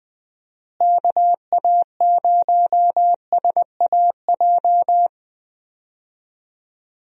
Morse code, heard as a keyed tone, about 21 wpm, 700 Hz. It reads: KA0SAJ
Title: USA Call Signs - 6 Characters - Repeated 3x - 20wpm